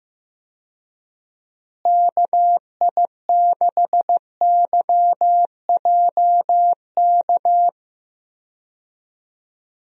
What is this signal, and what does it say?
Morse code, heard as a keyed tone, about 15 wpm, 700 Hz. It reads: KI6YJK
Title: USA Call Signs - 6 Characters - Repeated 3x - 15wpm